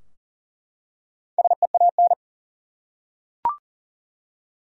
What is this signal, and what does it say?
Morse code, heard as a keyed tone, about 40 wpm, 700 Hz. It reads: SEAN
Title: USA Common Names - Max 4 Letters - 40wpm